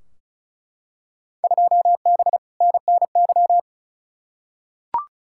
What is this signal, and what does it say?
Morse code, heard as a keyed tone, about 36 wpm, 700 Hz. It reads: 2B NNY
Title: Field Day Exchanges Encore - 35wpm